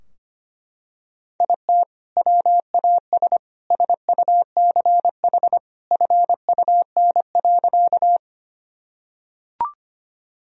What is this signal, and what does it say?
Morse code, heard as a keyed tone, about 25 wpm, 700 Hz. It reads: IT WAS SUCH FUN.
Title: Sentences from Top 1000 Words - Review Part 4 - 25wpm